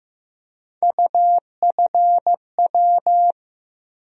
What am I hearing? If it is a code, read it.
Morse code, heard as a keyed tone, about 15 wpm, 700 Hz. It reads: UFW
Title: USA Call Sign Postfixes - 3 Letters 15wpm